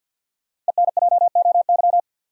Morse code, heard as a keyed tone, about 51 wpm, 700 Hz. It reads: ENJOY
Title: Sentences from Top 1000 Words - Review Part 4 - New Word Drill - 50wpm